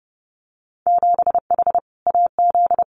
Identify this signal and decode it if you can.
Morse code, heard as a keyed tone, about 30 wpm, 700 Hz. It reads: 7H AZ
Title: Winter Field Day Exchanges Encore 30wpm